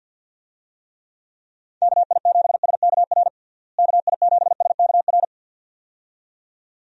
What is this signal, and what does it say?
Morse code, heard as a keyed tone, about 51 wpm, 700 Hz. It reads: KI7SKR KI7SKR
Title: USA Call Signs - 6 Characters - Repeated 2x - 50wpm